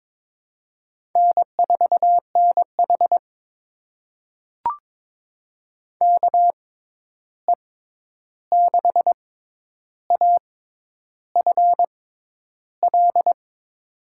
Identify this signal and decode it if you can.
Morse code, heard as a keyed tone, about 22 wpm, spaced as for 10 wpm, 700 Hz. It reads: N4NH KE6AFL
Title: USA Call Signs - 6x Character Spacing - 22wpm